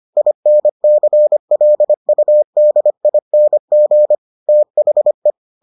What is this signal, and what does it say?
Morse code, heard as a keyed tone, about 25 wpm, 600 Hz. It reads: INCLUDING THE